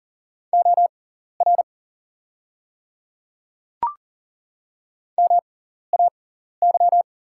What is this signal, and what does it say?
Morse code, heard as a keyed tone, about 40 wpm, spaced as for 14 wpm, 690 Hz. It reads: OR MAY
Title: 6x Character Spacing - Top 100 Words - 40wpm